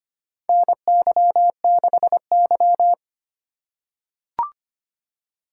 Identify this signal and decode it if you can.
Morse code, heard as a keyed tone, about 25 wpm, 700 Hz. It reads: NY6Y
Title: K1USN SST QSO Practice - 25wpm